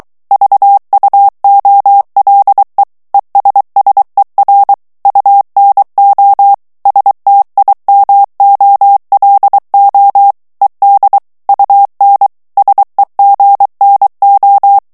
Morse code, heard as a keyed tone, about 23 wpm, 800 Hz. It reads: VUOLE ESSER UNO STIMOLO ED UN SEGNO
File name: morse mp3 / ITALIEN texte clair 1 vitesse 22 mots minute